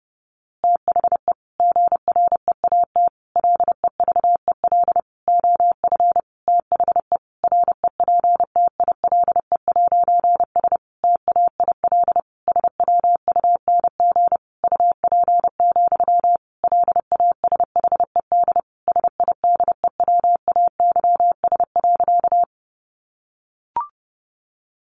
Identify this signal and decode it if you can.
Morse code, heard as a keyed tone, about 30 wpm, 700 Hz. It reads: THE GREAT LEVEL OF THE REPTILE'S TAIL SWUNG UP, LASHED SIDEWAYS.
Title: A Sound Of Thunder 30wpm